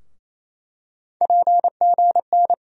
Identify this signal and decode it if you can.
Morse code, heard as a keyed tone, about 28 wpm, 700 Hz. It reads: PGN
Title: USA Call Sign Postfixes - 3 Letters 28wpm